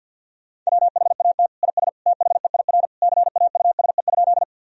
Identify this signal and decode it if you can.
Morse code, heard as a keyed tone, about 53 wpm, 700 Hz. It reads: WHAT IS THEIR CAUSE?
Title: Sentences from Top 1000 Words - Review Part 1 - 50wpm